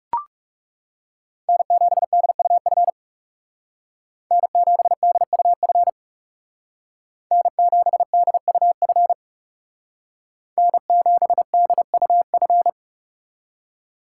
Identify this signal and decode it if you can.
Morse code, heard as a keyed tone, about 38 wpm, 700 Hz. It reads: N7DUF N7DUF N7DUF N7DUF